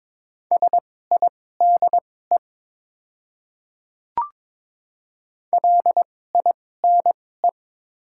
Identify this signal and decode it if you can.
Morse code, heard as a keyed tone, about 21 wpm, spaced as for 16 wpm, 700 Hz. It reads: SIDE LINE
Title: 2x Character Spacing - Top 200 Words - 22wpm